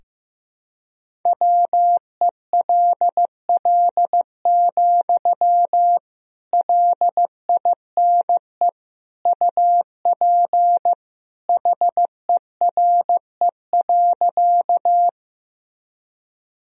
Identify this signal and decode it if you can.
Morse code, heard as a keyed tone, about 16 wpm, 700 Hz. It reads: WELL, LINE UP HERE.
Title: Sentences from Top 1000 Words - Review Part 1 - 15wpm